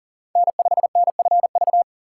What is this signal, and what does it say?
Morse code, heard as a keyed tone, about 40 wpm, 700 Hz. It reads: N5NFV
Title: USA Call Signs - 5 Characters - 6x Character Spacing - 40wpm